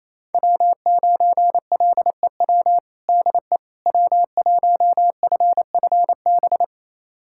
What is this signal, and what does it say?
Morse code, heard as a keyed tone, about 28 wpm, 700 Hz. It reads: W9LEW DE W1FFB
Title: USA Callsign DE Callsign - 5 Characters - 28wpm